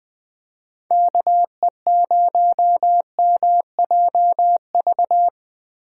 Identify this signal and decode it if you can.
Morse code, heard as a keyed tone, about 20 wpm, 700 Hz. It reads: KE0MJV